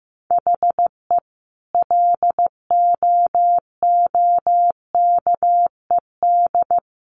Morse code, heard as a keyed tone, about 15 wpm, 700 Hz. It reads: HE LOOKED
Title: A Sound Of Thunder 15wpm